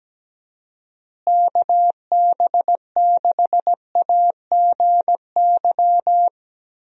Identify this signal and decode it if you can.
Morse code, heard as a keyed tone, about 17 wpm, 700 Hz. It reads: KB6AGY
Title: USA Call Signs - 6 Characters - Repeated 3x - 17wpm